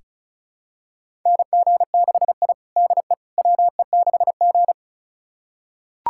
Morse code, heard as a keyed tone, about 34 wpm, 700 Hz. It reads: NG6I DE WE6G
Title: USA Callsign DE Callsign - 4 Characters - 35wpm